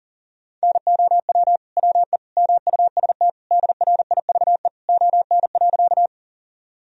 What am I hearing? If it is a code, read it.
Morse code, heard as a keyed tone, about 40 wpm, 700 Hz. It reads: NOW WE MUST DRIVE ON.
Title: Sentences from Top 1000 Words - Review Part 2 - 40wpm